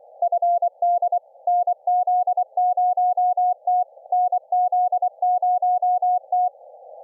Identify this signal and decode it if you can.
Morse code, heard as a keyed tone, about 24 wpm, 690 Hz. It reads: FD NZ0T NZ0T